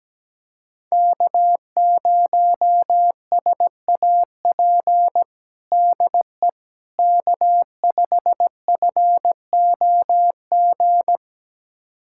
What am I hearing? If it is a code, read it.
Morse code, heard as a keyed tone, about 17 wpm, 700 Hz. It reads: K0SAP DE K5FOG
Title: USA Callsign DE Callsign - 5 Characters - 17wpm